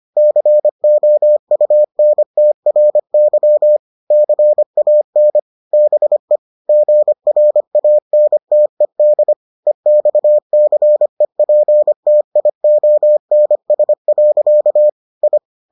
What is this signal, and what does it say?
Morse code, heard as a keyed tone, about 25 wpm, 600 Hz. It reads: COUNTRY CAN BE GRANTED EXCEPTIONS. I